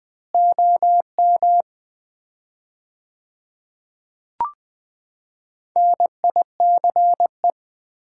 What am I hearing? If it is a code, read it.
Morse code, heard as a keyed tone, about 21 wpm, 700 Hz. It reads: OM NICE
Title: Common QSO Phrases - 4x Word-Spacing - 20wpm